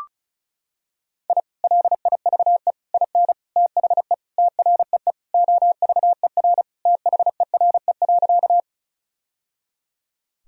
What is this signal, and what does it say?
Morse code, heard as a keyed tone, about 36 wpm, 700 Hz. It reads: I LIVE IN THE TREE OVER THERE.
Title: Sentences from Top 1000 Words - Review Part 1 - 35wpm